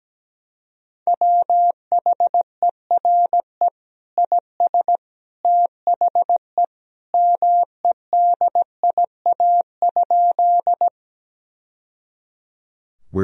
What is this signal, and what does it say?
Morse code, heard as a keyed tone, about 17 wpm, 700 Hz. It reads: WHERE IS THE MEDIA?
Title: Sentences from Top 1000 Words - Review Part 4 - 17wpm